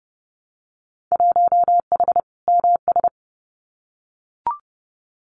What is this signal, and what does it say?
Morse code, heard as a keyed tone, about 32 wpm, 700 Hz. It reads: 1H MS
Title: Winter Field Day Exchanges 30wpm